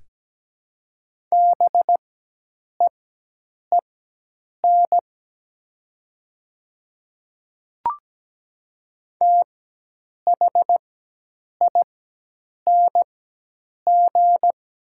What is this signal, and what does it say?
Morse code, heard as a keyed tone, about 17 wpm, spaced as for 8 wpm, 700 Hz. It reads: BEEN THING
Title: 4x Character Spacing - Top 300 Words - 17wpm